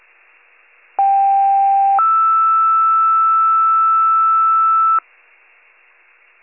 Two-tone page, heard as a one-second tone, then a three-second tone, about 790 then 1320 Hz. It is heard from 1.0 to 5.0 s.